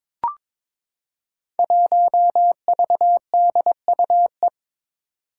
Morse code, heard as a keyed tone, about 21 wpm, 700 Hz. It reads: 1VDUE
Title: Five Challenging Letter-Number Combinations 22wpm